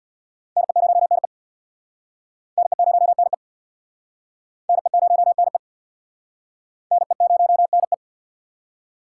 Morse code, heard as a keyed tone, about 66 wpm, 700 Hz. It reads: NE0NE NE0NE NE0NE NE0NE